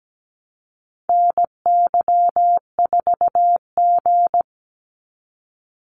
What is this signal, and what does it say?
Morse code, heard as a keyed tone, about 17 wpm, 700 Hz. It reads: NY4G
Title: NA SOTA Activator Callsigns 17wpm